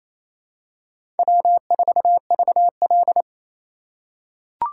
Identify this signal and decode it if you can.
Morse code, heard as a keyed tone, about 28 wpm, 700 Hz. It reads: W4VL
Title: USA Call Signs - 4 Characters - 4x Character Spacing - 28wpm